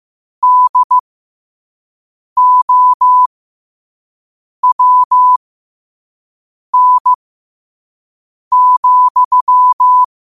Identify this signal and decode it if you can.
Morse code, heard as a keyed tone, about 15 wpm, spaced as for 5 wpm, 1000 Hz. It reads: DOWN,
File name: qotdfiles / 5wpm